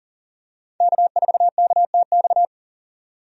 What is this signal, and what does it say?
Morse code, heard as a keyed tone, about 40 wpm, 700 Hz. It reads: K4KTX
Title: USA Call Signs - 5 Characters - 40wpm